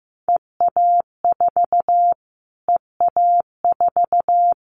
Morse code, heard as a keyed tone, about 15 wpm, 700 Hz. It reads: EA4 EA4
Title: Callsign Prefixes for Top 25 DXCC - Repeated 3x - 15wpm